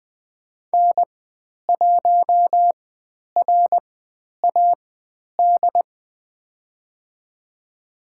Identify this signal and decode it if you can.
Morse code, heard as a keyed tone, about 20 wpm, spaced as for 10 wpm, 700 Hz. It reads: N1RAD
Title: 102 Lesson 17g - Letter u - Call Signs 20wpm